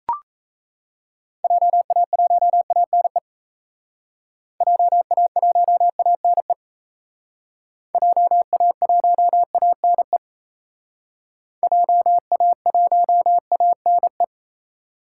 Morse code, heard as a keyed tone, about 37 wpm, 700 Hz. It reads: JA1ANE JA1ANE JA1ANE JA1ANE